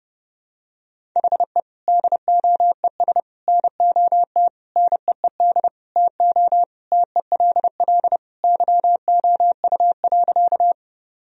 Morse code, heard as a keyed tone, about 30 wpm, 700 Hz. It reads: HE DOES NOT NEED TO TELL YOU.
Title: Sentences from Top 1000 Words - Review Part 1 - 30wpm